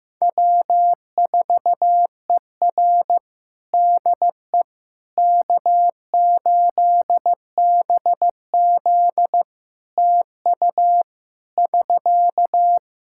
Morse code, heard as a keyed tone, about 15 wpm, 700 Hz. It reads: W4ER DE K8BZ TU <SK>
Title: Practice QSO - Exchange 4 - Ending - 15wpm